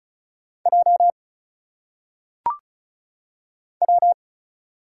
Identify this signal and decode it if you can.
Morse code, heard as a keyed tone, about 36 wpm, 700 Hz. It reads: J W
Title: Single Letters - Phonetic Pronunciation 35wpm